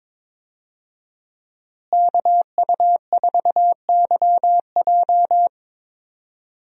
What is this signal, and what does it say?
Morse code, heard as a keyed tone, about 22 wpm, 700 Hz. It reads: KU4YJ
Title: USA Call Signs - 5 Characters - Repeated 2x - 22wpm